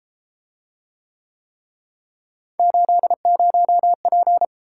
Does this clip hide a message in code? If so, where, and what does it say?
Yes: Morse code from 2.6 to 4.5 s, heard as a keyed tone, about 33 wpm, 700 Hz. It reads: 80P